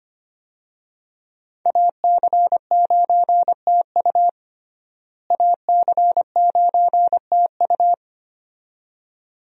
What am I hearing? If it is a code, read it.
Morse code, heard as a keyed tone, about 25 wpm, 700 Hz. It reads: AC9TU AC9TU
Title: USA Call Signs - 5 Characters - Repeated 2x - 25wpm